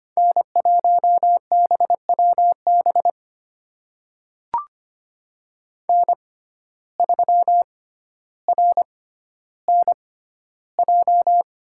Morse code, heard as a keyed tone, about 25 wpm, spaced as for 13 wpm, 700 Hz. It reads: N1BWB N3RNJ
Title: USA Call Signs - 5 Characters - 6x Character Spacing - 25wpm